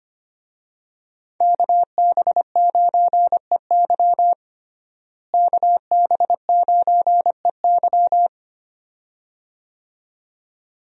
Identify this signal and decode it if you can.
Morse code, heard as a keyed tone, about 25 wpm, 700 Hz. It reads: KB9EY KB9EY
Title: USA Call Signs - 5 Characters - Repeated 2x - 25wpm